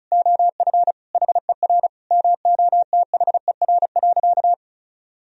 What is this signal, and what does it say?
Morse code, heard as a keyed tone, about 34 wpm, 700 Hz. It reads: OF HER MOTHER.
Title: Sentences from Top 1000 Words - Review Part 4 - 35wpm